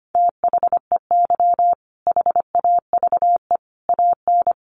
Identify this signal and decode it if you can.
Morse code, heard as a keyed tone, about 25 wpm, 700 Hz. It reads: THEY HAVE AN
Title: Sentences from Top 1000 Words 25wpm